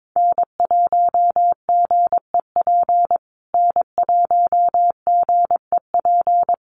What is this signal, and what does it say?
Morse code, heard as a keyed tone, about 22 wpm, 700 Hz. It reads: N1GEP N1GEP
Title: USA Call Signs Repeated 3x 22wpm